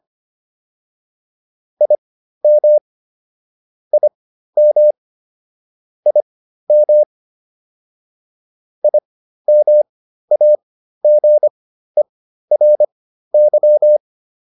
Morse code, heard as a keyed tone, about 25 wpm, spaced as for 13 wpm, 600 Hz. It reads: IM IM IM IMAGERY